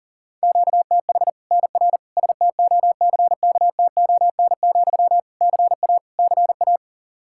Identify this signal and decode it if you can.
Morse code, heard as a keyed tone, about 40 wpm, 700 Hz. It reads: QTH NR STOCKTON, CA CA
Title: Practice QSO - Exchange 1 - Beginning - 40wpm